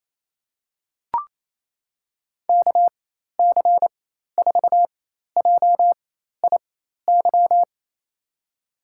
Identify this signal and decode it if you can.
Morse code, heard as a keyed tone, about 28 wpm, spaced as for 13 wpm, 700 Hz. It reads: KC4JIY
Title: USA Call Signs - 4x Character Spacing - 28wpm